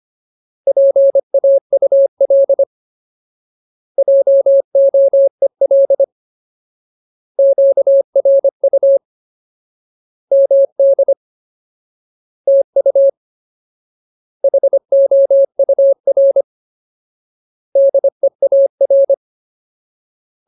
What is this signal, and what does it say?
Morse code, heard as a keyed tone, about 25 wpm, 550 Hz. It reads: PAUL JOEL QRU MD TU HOUR DEAR
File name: Ultimate Word List / 25WPM Ultimate